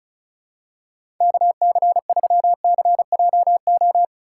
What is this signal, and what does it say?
Morse code, heard as a keyed tone, about 36 wpm, 700 Hz. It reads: KC3CJO
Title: USA Call Signs - 6 Characters - 4x Character Spacing - 35wpm